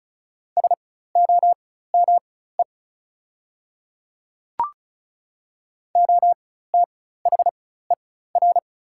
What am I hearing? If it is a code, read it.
Morse code, heard as a keyed tone, about 36 wpm, spaced as for 16 wpm, 690 Hz. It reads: SOME OTHER
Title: 4x Character Spacing - Top 100 Words - 35wpm